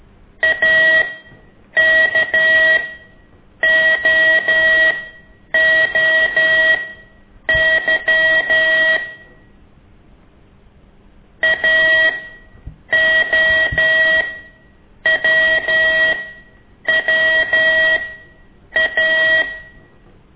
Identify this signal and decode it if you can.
Morse code, heard as a keyed tone, about 13 wpm, spaced as for 8 wpm, 1900 Hz. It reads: AKOOY AOWWA
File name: ik1weg2 / CORSI CW